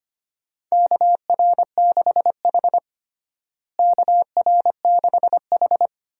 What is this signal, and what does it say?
Morse code, heard as a keyed tone, about 25 wpm, 700 Hz. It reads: KR6H KR6H
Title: USA Call Signs - 4 Characters - Repeated 3x - 25wpm